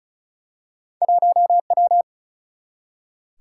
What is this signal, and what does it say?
Morse code, heard as a keyed tone, about 36 wpm, 700 Hz. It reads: 1W
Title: Single-Letter Single-Number 35wpm